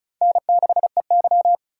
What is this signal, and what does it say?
Morse code, heard as a keyed tone, about 36 wpm, 700 Hz. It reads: N6EY
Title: USA Call Signs - 4 Characters - 2x Character Spacing - 35wpm